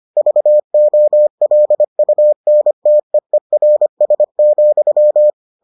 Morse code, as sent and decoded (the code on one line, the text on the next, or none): ...- --- .-.. ..- -. - . .-. ... --..--
VOLUNTEERS,